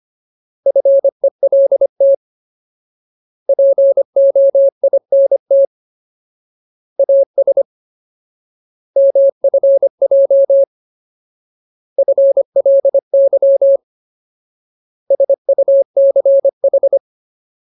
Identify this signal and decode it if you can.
Morse code, heard as a keyed tone, about 25 wpm, 550 Hz. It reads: FELT POINT AS MFJ FLY SUCH
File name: Ultimate Word List / 25WPM Ultimate